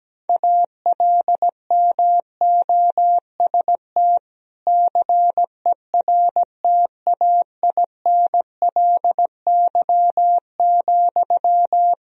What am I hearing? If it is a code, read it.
Morse code, heard as a keyed tone, about 17 wpm, 700 Hz. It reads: ALMOST CERTAINLY,
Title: Sentences from Top 1000 Words - Review Part 4 - 17wpm